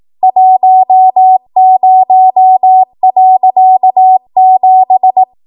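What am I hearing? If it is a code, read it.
Morse code, heard as a keyed tone, about 18 wpm, 750 Hz. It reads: 10.7
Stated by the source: ARRL 18 wpm Morse code practice archive transmission.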